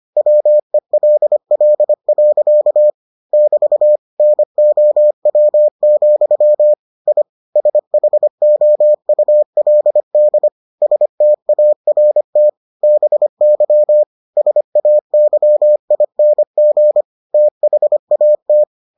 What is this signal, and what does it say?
Morse code, heard as a keyed tone, about 25 wpm, 600 Hz. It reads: WELL. <BT> NOW, I SHOULD START BY SAYING THAT